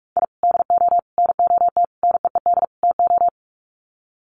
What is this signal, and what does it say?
Morse code, heard as a keyed tone, about 44 wpm, 700 Hz. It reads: I DO NOT NEED TO
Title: Sets of 5 Words 45wpm